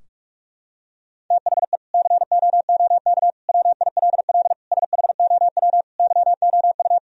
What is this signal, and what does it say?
Morse code, heard as a keyed tone, about 43 wpm, 700 Hz. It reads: THE COOK WILL SHOW YOU
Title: Sentences from Top 1000 Words - Review Part 4 - 45wpm